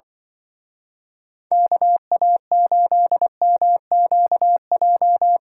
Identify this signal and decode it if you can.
Morse code, heard as a keyed tone, about 24 wpm, 700 Hz. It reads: KA8MQJ
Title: Speed Racing - USA 2x3 Call Signs 20wpm